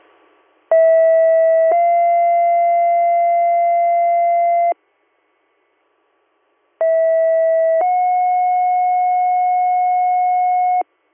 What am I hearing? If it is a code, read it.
Two-tone page, heard as a one-second tone, then a three-second tone, about 650 then 690 Hz, about 650 then 730 Hz. It is heard from 0.7 to 10.8 s.